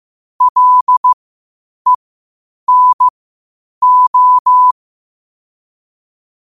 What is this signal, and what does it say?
Morse code, heard as a keyed tone, about 15 wpm, spaced as for 8 wpm, 1000 Hz. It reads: LENO
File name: qotdfiles / 8wpm